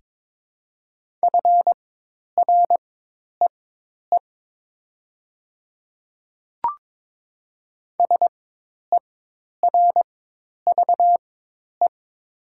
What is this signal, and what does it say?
Morse code, heard as a keyed tone, about 22 wpm, spaced as for 10 wpm, 700 Hz. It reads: FREE SERVE
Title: 4x Character Spacing - Top 500 Words - 22wpm